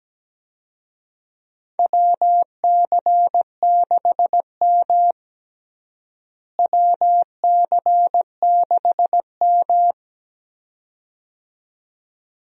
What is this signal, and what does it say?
Morse code, heard as a keyed tone, about 17 wpm, 700 Hz. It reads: WC6M WC6M
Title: USA Call Signs - 4 Characters - Repeated 2x - 17wpm